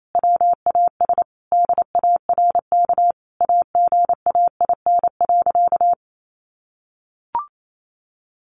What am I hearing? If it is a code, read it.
Morse code, heard as a keyed tone, about 28 wpm, 700 Hz. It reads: WAS DARK AGAIN.
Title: Sentences from Top 500 Words 28wpm